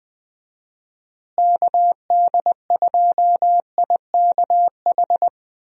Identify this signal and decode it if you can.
Morse code, heard as a keyed tone, about 21 wpm, 700 Hz. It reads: KD2IKH
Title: USA Call Signs - 6 Characters - Repeated 2x - 20wpm